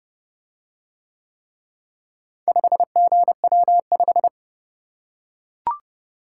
Morse code, heard as a keyed tone, about 29 wpm, 700 Hz. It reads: 5GW5